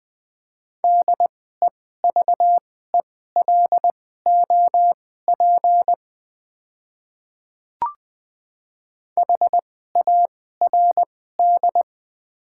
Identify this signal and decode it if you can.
Morse code, heard as a keyed tone, about 20 wpm, spaced as for 14 wpm, 700 Hz. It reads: DEVELOP HARD
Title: 2x Character Spacing - Top 500 Words - 20wpm